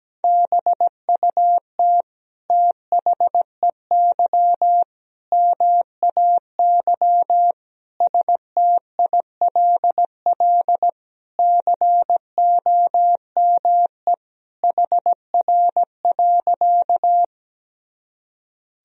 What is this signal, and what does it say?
Morse code, heard as a keyed tone, about 17 wpm, 700 Hz. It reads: BUT THEY MAY STILL COME HR.
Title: Sentences from Top 300 Words and Abbreviations 17wpm